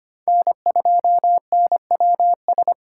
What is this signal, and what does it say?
Morse code, heard as a keyed tone, about 32 wpm, 700 Hz. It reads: N2NWS